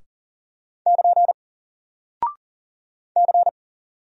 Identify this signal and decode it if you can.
Morse code, heard as a keyed tone, about 40 wpm, 700 Hz. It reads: <KN> C
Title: Single Character (Letter:Number:Punctuation:Prosign) 40wpm